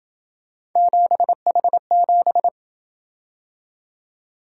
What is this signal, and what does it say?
Morse code, heard as a keyed tone, about 28 wpm, 700 Hz. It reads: 7H7